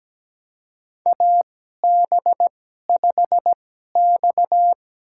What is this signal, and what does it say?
Morse code, heard as a keyed tone, about 17 wpm, spaced as for 12 wpm, 700 Hz. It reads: AB5X
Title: USA Call Signs - 4 Characters - 2x Character Spacing - 17wpm